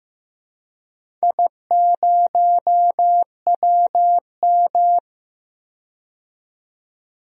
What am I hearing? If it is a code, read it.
Morse code, heard as a keyed tone, about 15 wpm, 700 Hz. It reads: I0WM
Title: DXCC-100 Call Signs - 4 Characters - 15wpm